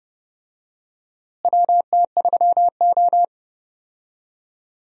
Morse code, heard as a keyed tone, about 30 wpm, 700 Hz. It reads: WT3O
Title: USA Call Signs - 4 Characters - Repeated 2x - 30wpm